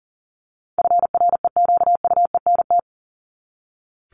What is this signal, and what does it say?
Morse code, heard as a keyed tone, about 40 wpm, 700 Hz. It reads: FREQUENT
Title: LY Suffix - Top 100 Words with Base Words 40wpm